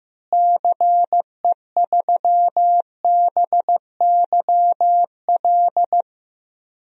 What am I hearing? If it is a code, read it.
Morse code, heard as a keyed tone, about 15 wpm, 700 Hz. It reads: CE3BYL
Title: DXCC-100 Call Signs - 6 Characters - Repeated 3x - 15wpm